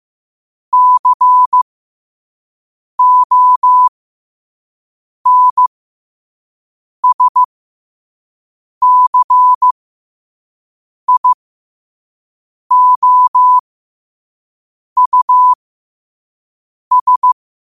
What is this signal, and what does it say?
Morse code, heard as a keyed tone, about 15 wpm, spaced as for 5 wpm, 1000 Hz. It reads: CONSCIOUS
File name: qotdfiles / 5wpm